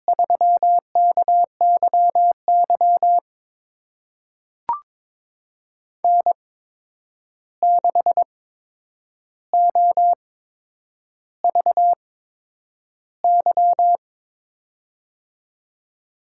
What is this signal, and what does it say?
Morse code, heard as a keyed tone, about 22 wpm, spaced as for 9 wpm, 700 Hz. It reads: 3KYY N6OVY
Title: USA Call Signs - 5 Characters - 8x Character Spacing - 22wpm